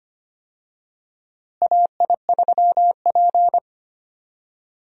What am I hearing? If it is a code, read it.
Morse code, heard as a keyed tone, about 25 wpm, 700 Hz. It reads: AI3P
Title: USA Call Signs - 4 Characters - Repeated 3x - 25wpm